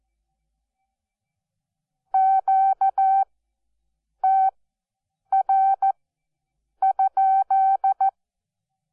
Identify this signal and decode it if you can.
Morse code, heard as a keyed tone, about 14 wpm, spaced as for 7 wpm, 770 Hz. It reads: QTR?
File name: Onlinelehrgang / morsen / DJ4UF Morsen12